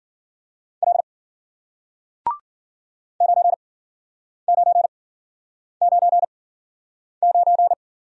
Morse code, heard as a keyed tone, about 53 wpm, spaced as for 9 wpm, 700 Hz. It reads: L 9999